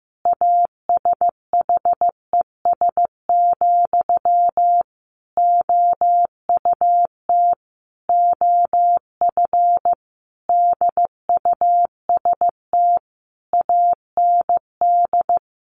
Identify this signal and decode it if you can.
Morse code, heard as a keyed tone, about 15 wpm, 700 Hz. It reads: ASHES, OUT OF DUST AND
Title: A Sound Of Thunder 15wpm